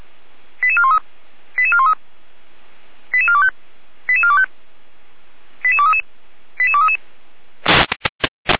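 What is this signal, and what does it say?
Five-tone selective call (ZVEI), heard as quick runs of five tones, tones of about 70 ms. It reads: 88413 88413 88425 88425 88233 88233
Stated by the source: ZVEI five-tone selective call received on SDR.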